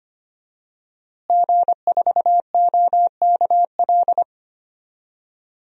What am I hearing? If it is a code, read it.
Morse code, heard as a keyed tone, about 25 wpm, 700 Hz. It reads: G4OKL
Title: DXCC-100 Call Signs - 5 Characters - Repeated 3x - 25wpm